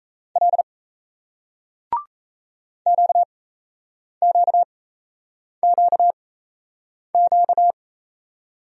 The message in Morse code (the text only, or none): L QQQQ